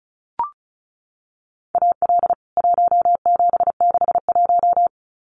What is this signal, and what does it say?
Morse code, heard as a keyed tone, about 34 wpm, 700 Hz. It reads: AL 1761
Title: CWT Exchange 35wpm